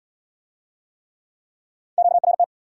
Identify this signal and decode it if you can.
Morse code, heard as a keyed tone, about 75 wpm, 700 Hz. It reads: QRT